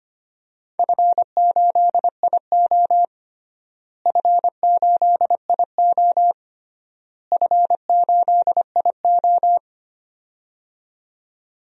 Morse code, heard as a keyed tone, about 25 wpm, 700 Hz. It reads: F8IO F8IO F8IO
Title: DXCC-100 Call Signs - 4 Characters - Repeated 3x - 25wpm